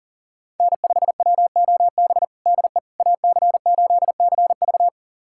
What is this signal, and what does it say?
Morse code, heard as a keyed tone, about 40 wpm, 700 Hz. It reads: N5WOB DE AC8CV
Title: USA Callsign DE Callsign - 5 Characters - 40wpm